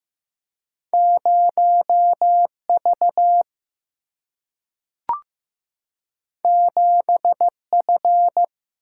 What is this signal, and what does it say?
Morse code, heard as a keyed tone, about 15 wpm, 700 Hz. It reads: 0V 7F
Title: Single-Letter Single-Number 15wpm